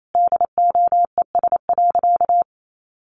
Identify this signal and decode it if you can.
Morse code, heard as a keyed tone, about 28 wpm, 700 Hz. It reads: DOES.
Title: Sentences from Top 500 Words 28wpm